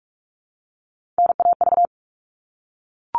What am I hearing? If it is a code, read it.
Morse code, heard as a keyed tone, about 44 wpm, 700 Hz. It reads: NAV
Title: All Cut Numbers - 3-Digit 45wpm